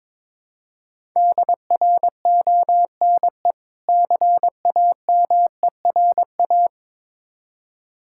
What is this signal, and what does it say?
Morse code, heard as a keyed tone, about 22 wpm, 690 Hz. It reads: DRONE CAMERA
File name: Difficult Words with Hints - 22wpm